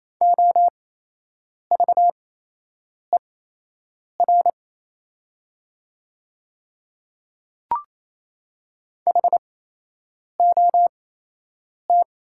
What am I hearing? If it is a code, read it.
Morse code, heard as a keyed tone, about 28 wpm, spaced as for 8 wpm, 700 Hz. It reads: OVER HOT
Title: 8x Character Spacing - Top 100 Words - 28wpm